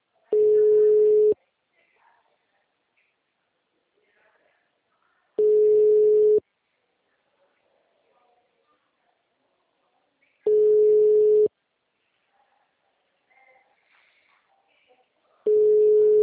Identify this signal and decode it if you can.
Ringback tone (Europe, 425 Hz), with speech.
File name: out- -2017- - - .4